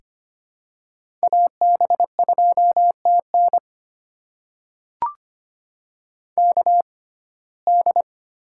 Morse code, heard as a keyed tone, about 25 wpm, spaced as for 18 wpm, 700 Hz. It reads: AB2TN KD
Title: USA Call Signs - 6x Character Spacing - 25wpm